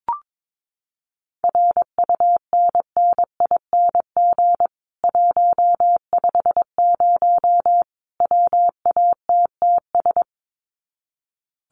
Morse code, heard as a keyed tone, about 22 wpm, 700 Hz. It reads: RUNNING 150 WATTS
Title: Common QSO Phrases 22wpm